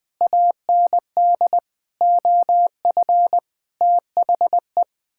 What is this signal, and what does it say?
Morse code, heard as a keyed tone, about 20 wpm, 700 Hz. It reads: AND OF THE
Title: Sets of 9 Words - 20wpm